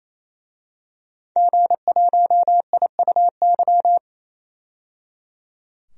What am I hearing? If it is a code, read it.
Morse code, heard as a keyed tone, about 28 wpm, 700 Hz. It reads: G1IUY
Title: DXCC-100 Call Signs - 5 Characters - 28wpm